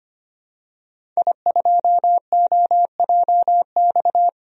Morse code, heard as a keyed tone, about 25 wpm, 700 Hz. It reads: I2OJX
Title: DXCC-100 Call Signs - 5 Characters - Repeated 3x - 25wpm